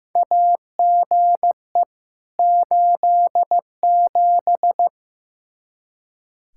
Morse code, heard as a keyed tone, about 15 wpm, 700 Hz. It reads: AGE 87
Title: K1USN SST QSO Practice 15wpm